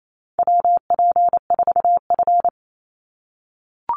Morse code, heard as a keyed tone, about 28 wpm, 700 Hz. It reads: WP4F
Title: Call Signs - 28wpm